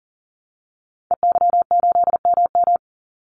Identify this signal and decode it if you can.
Morse code, heard as a keyed tone, about 40 wpm, 700 Hz. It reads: EY8MM
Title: Call Signs Repeated 2x - 40wpm